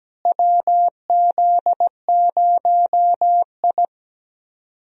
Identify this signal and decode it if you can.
Morse code, heard as a keyed tone, about 17 wpm, 700 Hz. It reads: WZ0I
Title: USA Call Signs - 4 Characters - Repeated 3x - 17wpm